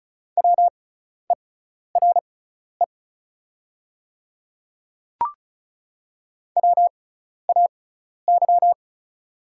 Morse code, heard as a keyed tone, about 36 wpm, spaced as for 12 wpm, 700 Hz. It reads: WERE WAY
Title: 6x Character Spacing - Top 100 Words - 35wpm